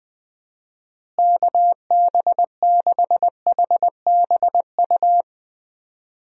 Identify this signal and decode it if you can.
Morse code, heard as a keyed tone, about 20 wpm, 700 Hz. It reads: KB6HBU
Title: USA Call Signs - 6 Characters - Repeated 2x - 20wpm